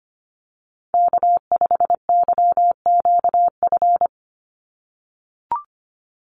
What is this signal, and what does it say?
Morse code, heard as a keyed tone, about 25 wpm, 700 Hz. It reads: K5YQF
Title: Call Signs - 25wpm